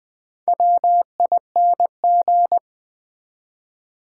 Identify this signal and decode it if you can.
Morse code, heard as a keyed tone, about 20 wpm, 700 Hz. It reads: WING
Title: Build A Word - 2 Letters At A Time - 4-Letter Words - 20wpm